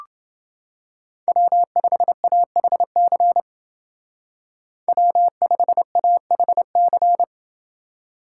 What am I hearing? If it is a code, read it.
Morse code, heard as a keyed tone, about 28 wpm, 700 Hz. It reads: W5AHC W5AHC